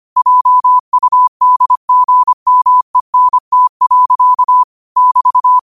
Morse code, heard as a keyed tone, about 25 wpm, 1000 Hz. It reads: JUDGMENT. <BT>